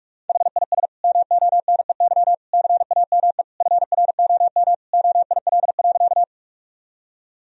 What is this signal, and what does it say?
Morse code, heard as a keyed tone, about 46 wpm, 700 Hz. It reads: HIS MONEY CAME FROM OIL.
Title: Sentences from Top 1000 Words - Review Part 4 - 45wpm